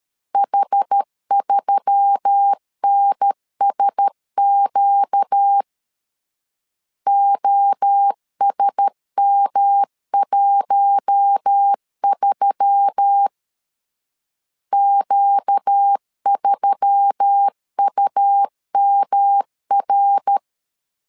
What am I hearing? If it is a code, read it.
Morse code, heard as a keyed tone, about 13 wpm, 800 Hz. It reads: H3NSQ OSM13 Q3UMR
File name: LEZIONI / 4 Q1S3 12wpm 2023